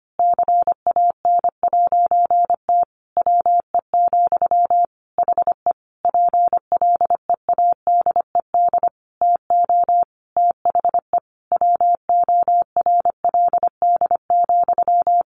Morse code, heard as a keyed tone, about 25 wpm, 700 Hz. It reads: CAN'T WE, HE PLEADED TO THE WORLD,
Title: A Sound Of Thunder 25wpm